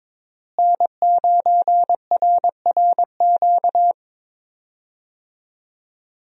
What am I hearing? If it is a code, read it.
Morse code, heard as a keyed tone, about 22 wpm, 700 Hz. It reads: N9RRQ
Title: USA Call Signs - 5 Characters - Repeated 2x - 22wpm